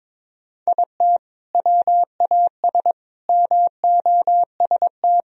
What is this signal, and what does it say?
Morse code, heard as a keyed tone, about 21 wpm, 700 Hz. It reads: IT WAS MOST
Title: Sentences from Top 1000 Words - Review Part 1 - 22wpm